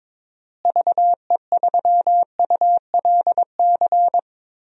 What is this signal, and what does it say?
Morse code, heard as a keyed tone, about 22 wpm, 700 Hz. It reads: VE3ULC
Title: DXCC-100 Call Signs - 6 Characters - Repeated 2x - 22wpm